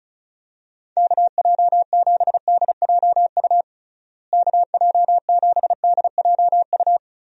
Morse code, heard as a keyed tone, about 36 wpm, 700 Hz. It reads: KJ7DJU KJ7DJU
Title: USA Call Signs - 6 Characters - Repeated 3x - 35wpm